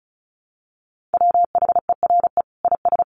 Morse code, heard as a keyed tone, about 34 wpm, 700 Hz. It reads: WHERE IS
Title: Sentences from Top 1300 Words 35wpm